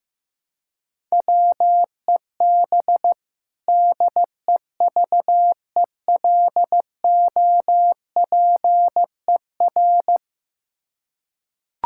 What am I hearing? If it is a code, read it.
Morse code, heard as a keyed tone, about 15 wpm, 700 Hz. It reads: WEB DEVELOPER
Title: Practice QSO - Exchange 3 - More Info - 15wpm